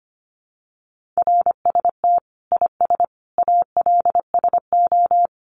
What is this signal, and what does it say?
Morse code, heard as a keyed tone, about 25 wpm, 700 Hz. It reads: RST IS ALSO
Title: Common QSO Phrases 25wpm